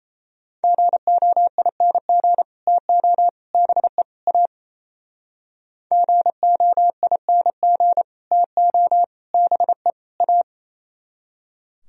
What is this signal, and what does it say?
Morse code, heard as a keyed tone, about 36 wpm, 700 Hz. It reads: GOING TO BE A GOING TO BE A